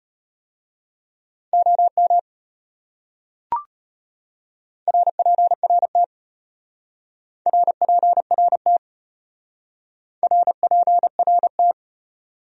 Morse code, heard as a keyed tone, about 31 wpm, 700 Hz. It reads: OM RPRT RPRT RPRT
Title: Speed-Racing Abbreviations - 25wpm